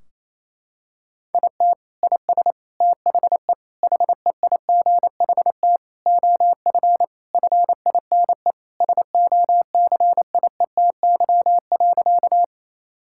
Morse code, heard as a keyed tone, about 28 wpm, 700 Hz. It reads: IT IS THE HEIGHT OF FINE SOCIETY.
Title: Sentences from Top 1600 Words 28wpm